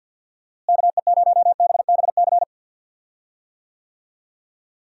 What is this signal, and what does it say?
Morse code, heard as a keyed tone, about 53 wpm, 700 Hz. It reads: KE0BBC
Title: USA Call Signs - 6 Characters - Repeated 2x - 50wpm